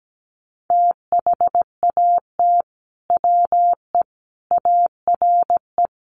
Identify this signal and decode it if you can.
Morse code, heard as a keyed tone, about 17 wpm, 700 Hz. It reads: THAT WE ARE
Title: Sets of 5 Words Encore 17wpm